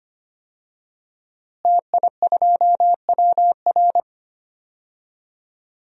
Morse code, heard as a keyed tone, about 25 wpm, 700 Hz. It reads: TI2WR
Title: DXCC-100 Call Signs - 5 Characters - Repeated 3x - 25wpm